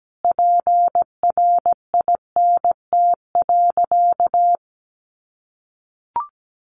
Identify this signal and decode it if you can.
Morse code, heard as a keyed tone, about 17 wpm, 700 Hz. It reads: PRINT.
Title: Sentences from Top 1400 Words 17wpm